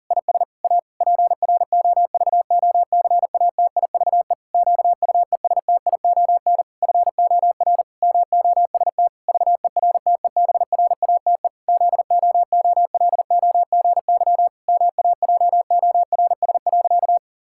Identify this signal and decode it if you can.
Morse code, heard as a keyed tone, about 40 wpm, 700 Hz. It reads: IS A PROVOCATIVE QUESTION FOR MOST VERTEBRATE ZOOLOGY MAJORS.
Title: Pangram Challenge v2 - 40wpm